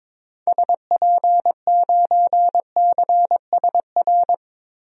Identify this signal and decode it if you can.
Morse code, heard as a keyed tone, about 22 wpm, 700 Hz. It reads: SP9CSR
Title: DXCC-100 Call Signs - 6 Characters - 22wpm